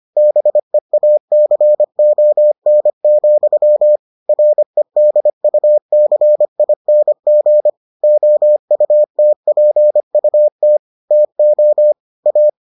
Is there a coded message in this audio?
Yes: Morse code from 0.2 to 12.5 s, heard as a keyed tone, about 25 wpm, 600 Hz. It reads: BEACON, REDUCING OUTPUT TO A